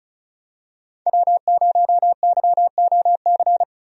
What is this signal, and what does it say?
Morse code, heard as a keyed tone, about 36 wpm, 700 Hz. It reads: W0YOC
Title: USA Call Signs - 5 Characters - 35wpm